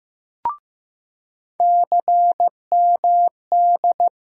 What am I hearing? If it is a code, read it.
Morse code, heard as a keyed tone, about 15 wpm, 700 Hz. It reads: CMD
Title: USA Call Sign Postfixes - 3 Letters 15wpm